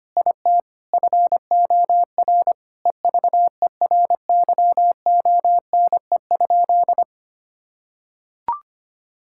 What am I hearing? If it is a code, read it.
Morse code, heard as a keyed tone, about 25 wpm, 700 Hz. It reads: IT FOR EVERYONE?
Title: Sentences from Top 1000 Words - Review Part 5 - 25wpm